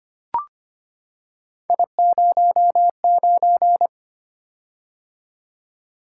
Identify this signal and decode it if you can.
Morse code, heard as a keyed tone, about 25 wpm, 700 Hz. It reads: I09
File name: Three Letter-Number Combinations - 25wpm